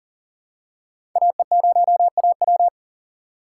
Morse code, heard as a keyed tone, about 40 wpm, 700 Hz. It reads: AE0AW
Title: USA Call Signs - 5 Characters - 3x Character Spacing - 40wpm